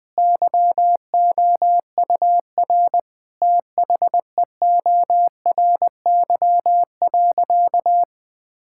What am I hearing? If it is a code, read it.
Morse code, heard as a keyed tone, about 20 wpm, 700 Hz. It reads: YOUR THEORY.
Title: Sentences from Top 1000 Words - Review Part 5 - 20wpm